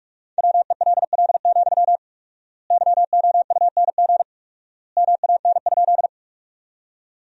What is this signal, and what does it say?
Morse code, heard as a keyed tone, about 43 wpm, 700 Hz. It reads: WELL, YOUNG MAN?
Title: 4x Word-Spacing - Sentences from Top 400 Words - 45wpm